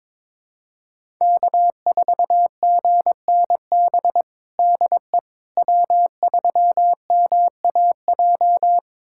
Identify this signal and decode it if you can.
Morse code, heard as a keyed tone, about 22 wpm, 700 Hz. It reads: K4GNB DE W3MAJ
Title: USA Callsign DE Callsign - 5 Characters - 22wpm